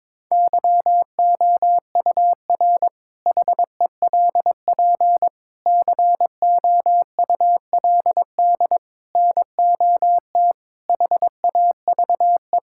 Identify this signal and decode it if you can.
Morse code, heard as a keyed tone, about 22 wpm, 700 Hz. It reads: YOUR HELP COULD NOT HAVE
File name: practice / 22wpm / Sentences with 8-Word Phrases - 22wpm